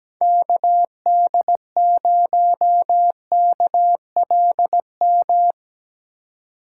Morse code, heard as a keyed tone, about 19 wpm, 700 Hz. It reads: KD0KLM
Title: Speed Racing - USA 2x3 Call Signs 17wpm